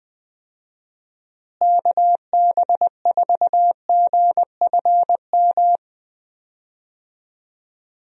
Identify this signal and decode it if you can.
Morse code, heard as a keyed tone, about 20 wpm, 700 Hz. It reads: KB4GFM
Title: USA Call Signs - 6 Characters - Repeated 2x - 20wpm